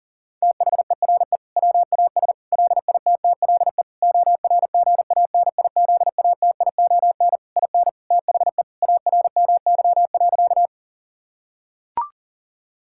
Morse code, heard as a keyed tone, about 40 wpm, 700 Hz. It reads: THERE WAS LITTLE ORGANIZATION IN THE ARMY.
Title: Sentences from Top 1000 Words - Review Part 5 - 40wpm